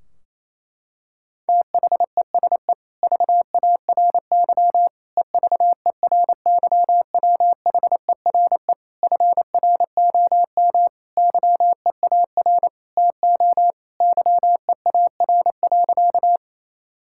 Morse code, heard as a keyed tone, about 28 wpm, 700 Hz. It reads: THESE VARY EVERYWHERE FROM YEAR TO YEAR.
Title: Sentences from Top 1600 Words 28wpm